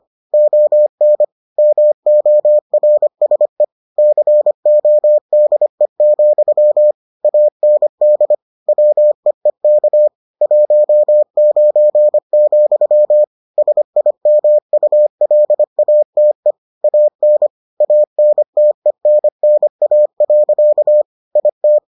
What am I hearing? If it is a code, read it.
Morse code, heard as a keyed tone, about 25 wpm, 600 Hz. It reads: ON MORSE CODE, AND WEEK 19, SIMULATE AN ANTENNA. IT